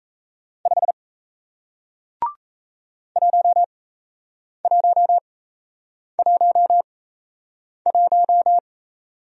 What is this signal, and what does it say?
Morse code, heard as a keyed tone, about 37 wpm, spaced as for 8 wpm, 700 Hz. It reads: 5 1111